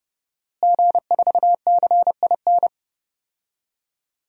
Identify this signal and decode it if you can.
Morse code, heard as a keyed tone, about 30 wpm, 700 Hz. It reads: G4CIN
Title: DXCC-100 Call Signs - 5 Characters - Repeated 2x - 30wpm